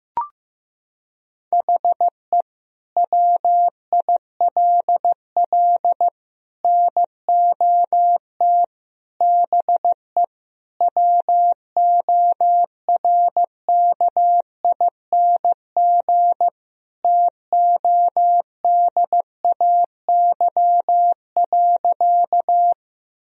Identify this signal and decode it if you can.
Morse code, heard as a keyed tone, about 15 wpm, 700 Hz. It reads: HE WILL NOT BE WORKING TODAY.
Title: Sentences from Top 1000 Words - Review Part 4 - 15wpm